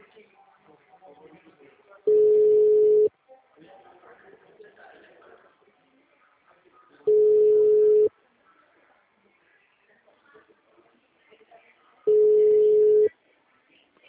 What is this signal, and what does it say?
Ringback tone (Europe, 425 Hz), with speech.
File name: out- -2003- - - .0